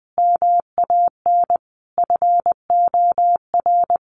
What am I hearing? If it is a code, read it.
Morse code, heard as a keyed tone, about 20 wpm, 700 Hz. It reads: MAN FOR
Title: A Sound Of Thunder 20wpm